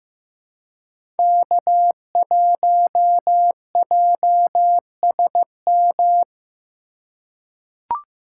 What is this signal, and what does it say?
Morse code, heard as a keyed tone, about 15 wpm, 700 Hz. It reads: K1JSM
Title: USA Call Signs - 5 Characters - 4x Character Spacing - 15wpm